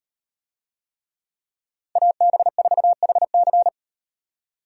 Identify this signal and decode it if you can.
Morse code, heard as a keyed tone, about 35 wpm, 700 Hz. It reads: AB4HC